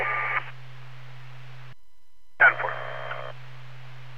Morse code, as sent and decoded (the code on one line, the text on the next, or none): none